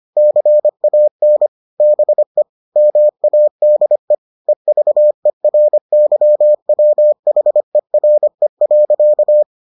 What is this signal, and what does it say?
Morse code, heard as a keyed tone, about 25 wpm, 600 Hz. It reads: CAN BE MADE EVERYWHERE.